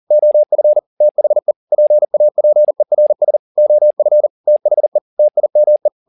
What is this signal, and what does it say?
Morse code, heard as a keyed tone, about 40 wpm, 600 Hz. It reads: OF THE PAPERS OF THE TIME
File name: the old man and the sea 2 40wpm